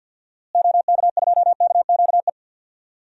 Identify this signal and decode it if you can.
Morse code, heard as a keyed tone, about 47 wpm, 700 Hz. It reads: OK2KQE
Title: DXCC-100 Call Signs - 6 Characters - Repeated 2x - 50wpm